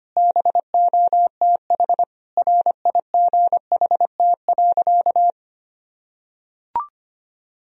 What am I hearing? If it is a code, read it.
Morse code, heard as a keyed tone, about 25 wpm, 700 Hz. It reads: BOTH RIGHT.
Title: Sentences from Top 1000 Words - Review Part 1 - 25wpm